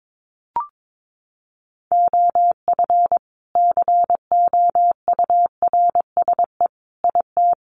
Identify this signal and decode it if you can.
Morse code, heard as a keyed tone, about 22 wpm, 700 Hz. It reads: OF COURSE IT
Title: Sentences from Top 500 Words 22wpm